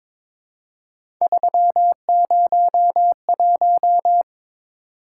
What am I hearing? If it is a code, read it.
Morse code, heard as a keyed tone, about 22 wpm, 700 Hz. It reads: 301
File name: practice / 22wpm / Three Letter-Number Combinations - 22wpm